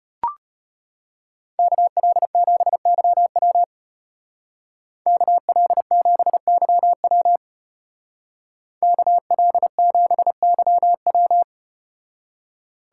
Morse code, heard as a keyed tone, about 34 wpm, 700 Hz. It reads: KL7YW KL7YW KL7YW